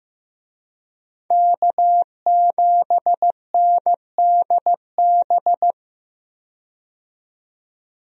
Morse code, heard as a keyed tone, about 15 wpm, 700 Hz. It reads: K7NDB
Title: USA Call Signs - 5 Characters - 15wpm